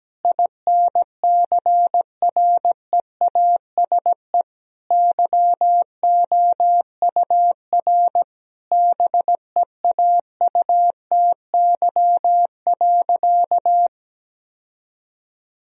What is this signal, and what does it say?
Morse code, heard as a keyed tone, about 17 wpm, 700 Hz. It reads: INCREASE YOUR BEAUTY.
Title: Sentences from Top 1000 Words - Review Part 4 - 17wpm